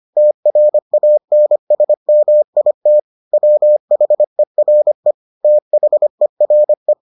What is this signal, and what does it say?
Morse code, heard as a keyed tone, about 25 wpm, 600 Hz. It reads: TRANSMIT WHERE THERE